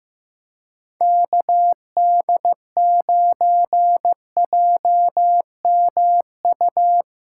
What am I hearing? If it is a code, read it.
Morse code, heard as a keyed tone, about 15 wpm, 700 Hz. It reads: KD9JMU
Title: USA Callsign DE Callsign 15wpm